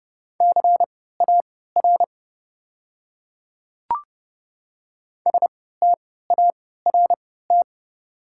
Morse code, heard as a keyed tone, about 30 wpm, spaced as for 17 wpm, 690 Hz. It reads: CAR START
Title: 3x Character Spacing - Top 500 Words - 30wpm